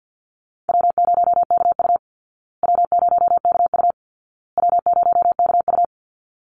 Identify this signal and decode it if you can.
Morse code, heard as a keyed tone, about 50 wpm, 700 Hz. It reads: W0KU W0KU W0KU
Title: Easy Call Signs Repeated 3x - 50wpm